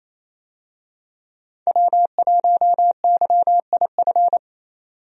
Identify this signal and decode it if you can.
Morse code, heard as a keyed tone, about 28 wpm, 700 Hz. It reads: W1YIF
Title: USA Call Signs - 5 Characters - 28wpm